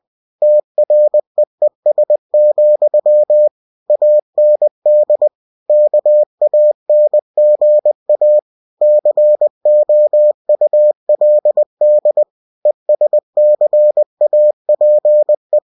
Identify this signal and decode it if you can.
Morse code, heard as a keyed tone, about 20 wpm, 600 Hz. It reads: TREES, AND KANGA COULD ESCAPE
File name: cw-books / winnie-the-pooh / winnie-20wpm-page-0091